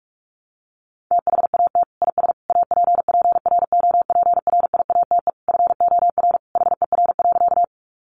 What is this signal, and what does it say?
Morse code, heard as a keyed tone, about 44 wpm, 700 Hz. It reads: THAT IS APPROPRIATE FOR HER.
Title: Sentences from Top 900 Words 45wpm